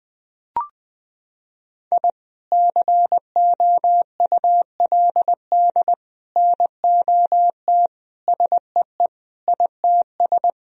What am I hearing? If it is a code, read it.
Morse code, heard as a keyed tone, about 20 wpm, 700 Hz. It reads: I COULD NOT SEE ITS
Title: Sentences from Top 1000 Words - Review Part 3 - 20wpm